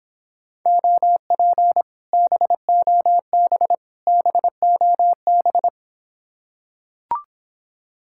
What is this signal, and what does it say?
Morse code, heard as a keyed tone, about 26 wpm, 700 Hz. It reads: OP BOB BOB